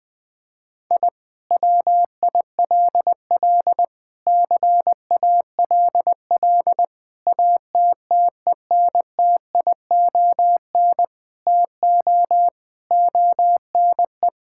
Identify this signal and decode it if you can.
Morse code, heard as a keyed tone, about 19 wpm, 700 Hz. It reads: I WILL CALL ATTENTION TO ONE
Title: Sentences from Top 1000 Words - Review Part 6 - 20wpm